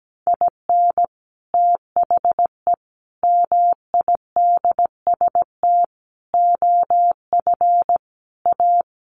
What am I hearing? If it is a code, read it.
Morse code, heard as a keyed tone, about 17 wpm, 700 Hz. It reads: IN THE MIDST OF A
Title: Sets of 5 Words 17wpm